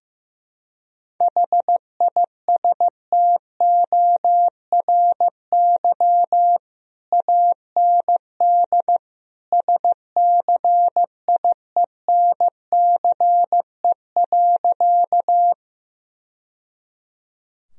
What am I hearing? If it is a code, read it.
Morse code, heard as a keyed tone, about 15 wpm, 700 Hz. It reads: HISTORY AND SCIENCE.
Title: Sentences from Top 1000 Words - Review Part 2 - 15wpm